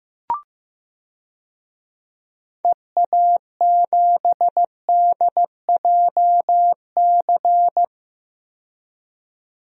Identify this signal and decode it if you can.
Morse code, heard as a keyed tone, about 16 wpm, 700 Hz. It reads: EA7DJC
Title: DXCC-100 Call Signs - 6 Characters - 15wpm